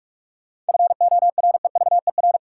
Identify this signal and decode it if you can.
Morse code, heard as a keyed tone, about 44 wpm, 700 Hz. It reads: FOREVER
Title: Sentences from Top 1600 Words - New Word Drill - 45wpm